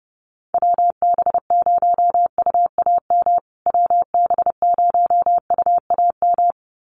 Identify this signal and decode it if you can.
Morse code, heard as a keyed tone, about 30 wpm, 700 Hz. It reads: WB0UAM WB0UAM
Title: USA Call Signs Repeated 2x 30wpm